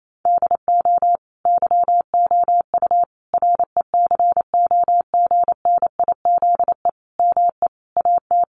Morse code, heard as a keyed tone, about 28 wpm, 700 Hz. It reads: DO YOU RECOGNIZE ME AT
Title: Sentences from Top 1400 Words 28wpm